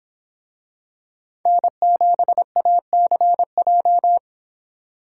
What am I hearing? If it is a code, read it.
Morse code, heard as a keyed tone, about 26 wpm, 700 Hz. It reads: N7ACJ